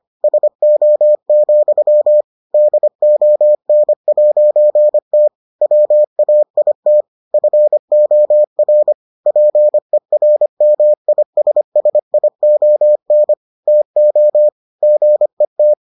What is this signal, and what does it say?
Morse code, heard as a keyed tone, about 25 wpm, 600 Hz. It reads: SO, DON'T WAIT FOR PERMISSION TO GET